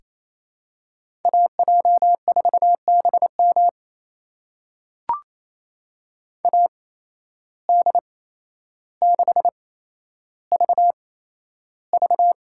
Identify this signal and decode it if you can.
Morse code, heard as a keyed tone, about 28 wpm, spaced as for 12 wpm, 700 Hz. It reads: AJ4BM AD6VV
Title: USA Call Signs - 5 Characters - 8x Character Spacing - 28wpm